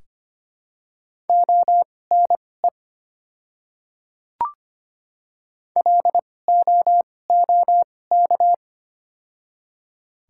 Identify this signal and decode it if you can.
Morse code, heard as a keyed tone, about 25 wpm, spaced as for 18 wpm, 700 Hz. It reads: ONE LOOK
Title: 2x Character Spacing - Top 300 Words - 25wpm